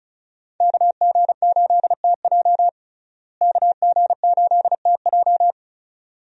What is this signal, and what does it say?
Morse code, heard as a keyed tone, about 36 wpm, 700 Hz. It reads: KG8TJ KG8TJ
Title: USA Call Signs - 5 Characters - Repeated 2x - 35wpm